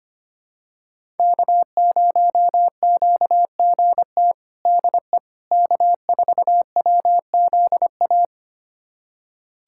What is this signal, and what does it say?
Morse code, heard as a keyed tone, about 25 wpm, 700 Hz. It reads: K0QGT DE K4WZA
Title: USA Callsign DE Callsign - 5 Characters - 25wpm